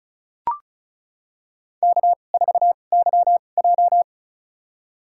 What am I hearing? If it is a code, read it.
Morse code, heard as a keyed tone, about 36 wpm, spaced as for 25 wpm, 700 Hz. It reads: K4YJ
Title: USA Call Signs - 4 Characters - 2x Character Spacing - 35wpm